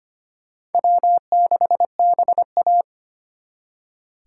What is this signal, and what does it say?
Morse code, heard as a keyed tone, about 32 wpm, 700 Hz. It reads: W6BA